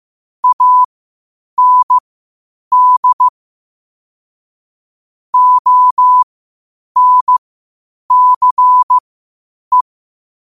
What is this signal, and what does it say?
Morse code, heard as a keyed tone, about 15 wpm, spaced as for 8 wpm, 1000 Hz. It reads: AND ONCE
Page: qotd-at-8-wpm